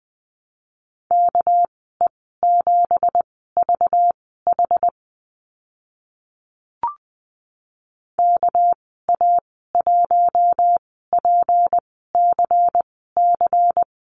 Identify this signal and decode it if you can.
Morse code, heard as a keyed tone, about 20 wpm, spaced as for 14 wpm, 700 Hz. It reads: KE7VH KA1PCC
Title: 213 Lesson 34g - Number 8 - Call Signs 20wpm